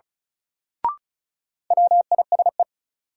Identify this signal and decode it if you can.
Morse code, heard as a keyed tone, about 36 wpm, 700 Hz. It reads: WISE
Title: Sentences from Top 1600 Words - New Word Drill - 35wpm